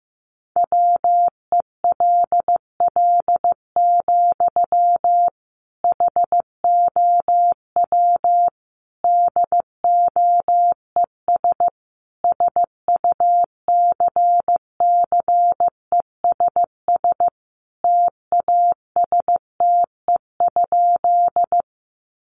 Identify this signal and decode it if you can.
Morse code, heard as a keyed tone, about 15 wpm, 700 Hz. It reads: WELL, HOW DOES SUCCESS TASTE?
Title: Sentences from Top 1100 Words 15wpm